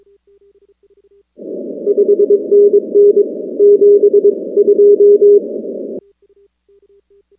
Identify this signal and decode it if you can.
Morse code, heard as a keyed tone, about 22 wpm, 410 Hz. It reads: 5NN 72
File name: contests / 2019-cw-wpx-kt5j-k5pi- -2330 0-19